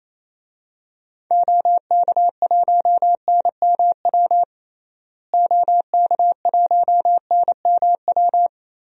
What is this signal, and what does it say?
Morse code, heard as a keyed tone, about 28 wpm, 700 Hz. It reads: OK1NMW OK1NMW
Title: DXCC-100 Call Signs - 6 Characters - Repeated 3x - 28wpm